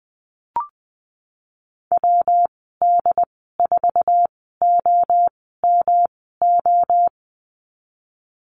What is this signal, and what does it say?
Morse code, heard as a keyed tone, about 20 wpm, spaced as for 15 wpm, 700 Hz. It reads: WD4OMO
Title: 140 Lesson 23g - Letter f - Call Signs 20wpm